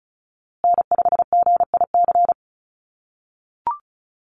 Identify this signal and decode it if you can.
Morse code, heard as a keyed tone, about 34 wpm, 700 Hz. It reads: N5GIC
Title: USA Call Signs Repeated 2x 35wpm